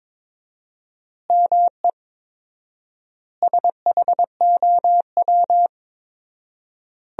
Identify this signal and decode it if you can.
Morse code, heard as a keyed tone, about 22 wpm, 700 Hz. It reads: ME SHOW
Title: 4x Word-Spacing - Sentences from Top 500 Words - 22wpm